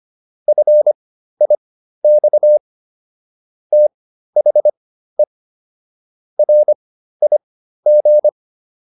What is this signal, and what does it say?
Morse code, heard as a keyed tone, about 25 wpm, spaced as for 13 wpm, 600 Hz. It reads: FIX THE RIG